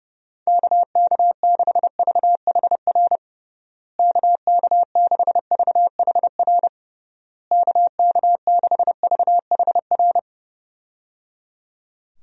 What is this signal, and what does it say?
Morse code, heard as a keyed tone, about 30 wpm, 700 Hz. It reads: KK6VHR KK6VHR KK6VHR
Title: USA Call Signs - 6 Characters - Repeated 3x - 30wpm